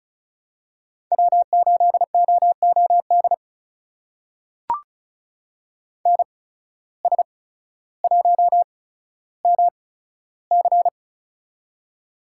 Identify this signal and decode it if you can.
Morse code, heard as a keyed tone, about 34 wpm, spaced as for 15 wpm, 700 Hz. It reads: W8OOD NS1MC
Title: USA Call Signs - 5 Characters - 8x Character Spacing - 35wpm